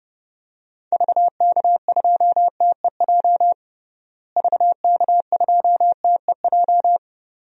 Morse code, heard as a keyed tone, about 30 wpm, 700 Hz. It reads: VK2TEJ VK2TEJ
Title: DXCC-100 Call Signs - 6 Characters - Repeated 2x - 30wpm